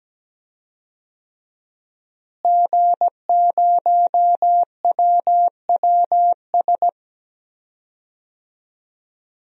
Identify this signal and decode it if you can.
Morse code, heard as a keyed tone, about 17 wpm, 700 Hz. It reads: G0WWS
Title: DXCC-100 Call Signs - 5 Characters - Repeated 2x - 17wpm